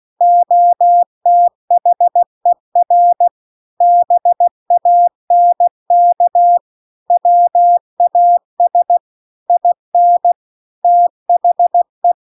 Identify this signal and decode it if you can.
Morse code, heard as a keyed tone, about 16 wpm, 700 Hz. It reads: OTHER BANK WAS IN THE